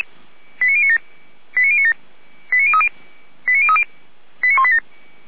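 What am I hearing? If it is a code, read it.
Five-tone selective call (ZVEI), heard as quick runs of five tones, tones of about 70 ms. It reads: 89097 89097 89030 89030 89187
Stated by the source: ZVEI five-tone selective call received on an SDR.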